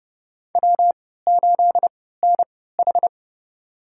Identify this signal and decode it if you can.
Morse code, heard as a keyed tone, about 31 wpm, spaced as for 17 wpm, 700 Hz. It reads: W8NH
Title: USA Call Signs - 3x Character Spacing - 30wpm